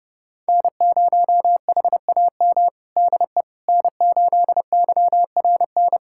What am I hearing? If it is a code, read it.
Morse code, heard as a keyed tone, about 30 wpm, 700 Hz. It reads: N0HAM DE N8YRN
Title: USA Callsign DE Callsign - 5 Characters - 30wpm